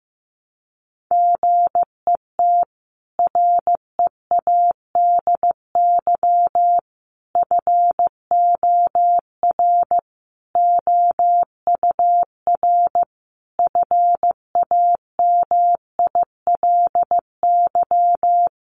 Morse code, heard as a keyed tone, about 15 wpm, 700 Hz. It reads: GET READY FOR OUR FAMILY
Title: Sentences from Top 1200 Words 15wpm